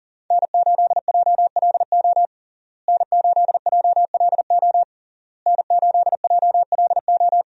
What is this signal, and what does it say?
Morse code, heard as a keyed tone, about 37 wpm, 700 Hz. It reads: N8JLO N8JLO N8JLO
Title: USA Call Signs - 5 Characters - Repeated 3x - 40wpm